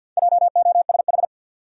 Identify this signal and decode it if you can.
Morse code, heard as a keyed tone, about 51 wpm, 700 Hz. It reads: JOSH
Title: USA Common Names - Max 5 Letters - 50wpm